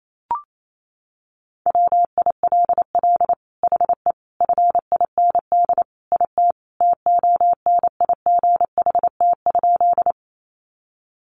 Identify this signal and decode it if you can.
Morse code, heard as a keyed tone, about 28 wpm, 700 Hz. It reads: WILL HE FIND IT TONIGHT?
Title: Sentences from Top 600 Words 28wpm